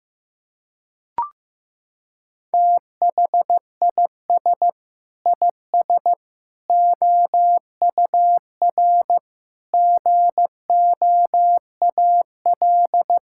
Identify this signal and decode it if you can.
Morse code, heard as a keyed tone, about 16 wpm, 700 Hz. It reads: THIS IS OUR GOAL
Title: Sentences from Top 1000 Words - Review Part 5 - 15wpm